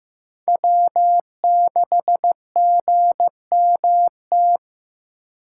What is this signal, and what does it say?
Morse code, heard as a keyed tone, about 15 wpm, 700 Hz. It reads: W6GMT
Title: USA Call Signs - 2x Character Spacing - 15wpm